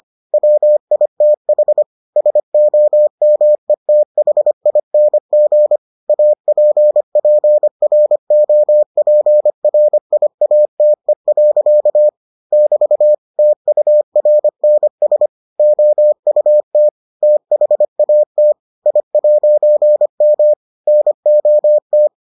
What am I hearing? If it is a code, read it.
Morse code, heard as a keyed tone, about 25 wpm, 600 Hz. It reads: WITH SOMETHING APPROPRIATE. <BT> TURNS OUT THAT I'M NOT